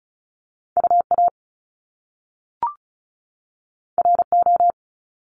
Morse code, heard as a keyed tone, about 36 wpm, 700 Hz. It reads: UA RO
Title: Top Two Letter Combinations in Words 35wpm